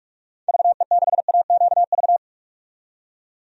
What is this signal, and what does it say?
Morse code, heard as a keyed tone, about 41 wpm, 700 Hz. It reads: VE6AQV
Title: DXCC-100 Call Signs - 6 Characters - 45wpm